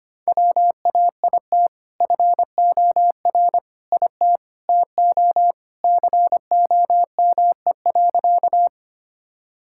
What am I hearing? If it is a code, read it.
Morse code, heard as a keyed tone, about 25 wpm, 700 Hz. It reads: WAIT FOR IT TO COME.
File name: practice / 25wpm / Sentences with 2-Word Phrases - 25wpm